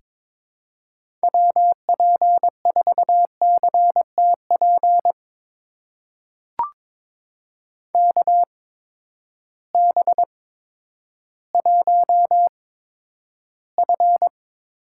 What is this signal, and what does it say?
Morse code, heard as a keyed tone, about 22 wpm, spaced as for 11 wpm, 700 Hz. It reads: WP4CTP KB1F
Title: USA Call Signs - 6 Characters - 8x Character Spacing - 22wpm